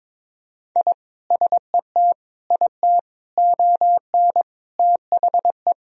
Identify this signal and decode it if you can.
Morse code, heard as a keyed tone, about 21 wpm, 700 Hz. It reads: I SET IT ON THE
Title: Sentences from Top 1000 Words - Review Part 2 - 22wpm